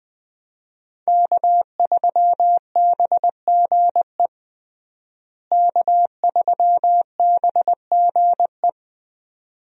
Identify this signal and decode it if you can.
Morse code, heard as a keyed tone, about 20 wpm, 700 Hz. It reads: K3BGE K3BGE
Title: USA Call Signs - 5 Characters - Repeated 2x - 20wpm